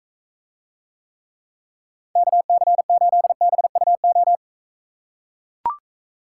Morse code, heard as a keyed tone, about 44 wpm, 700 Hz. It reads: KC8BUO